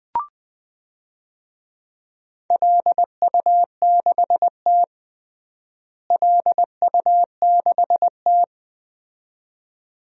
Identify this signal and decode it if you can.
Morse code, heard as a keyed tone, about 21 wpm, 700 Hz. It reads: LU6T LU6T
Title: DXCC-100 Call Signs - 4 Characters - Repeated 2x - 20wpm